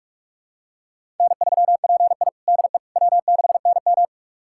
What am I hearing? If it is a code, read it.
Morse code, heard as a keyed tone, about 46 wpm, 700 Hz. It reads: N3PI DE W6NM
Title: USA Callsign DE Callsign - 4 Characters - 45wpm